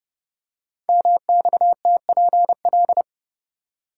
Morse code, heard as a keyed tone, about 30 wpm, 700 Hz. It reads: MXTPL
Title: Code Groups - 5 Letters - 30wpm